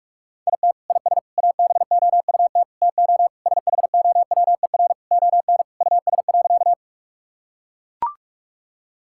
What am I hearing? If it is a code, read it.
Morse code, heard as a keyed tone, about 41 wpm, 700 Hz. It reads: IT IS ABOUT TO SHOWER ON US.
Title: Sentences from Top 1600 Words 45wpm